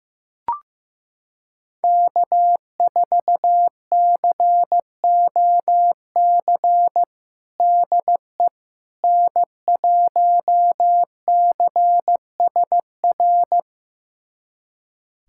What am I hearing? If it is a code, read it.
Morse code, heard as a keyed tone, about 15 wpm, 700 Hz. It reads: K4COC DE N1CSR
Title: USA Callsign DE Callsign 15wpm